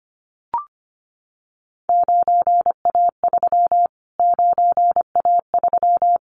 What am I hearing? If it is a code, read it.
Morse code, heard as a keyed tone, about 25 wpm, 700 Hz. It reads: 9A3 9A3
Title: Callsign Prefixes for Top 25 DXCC - Repeated 3x - 25wpm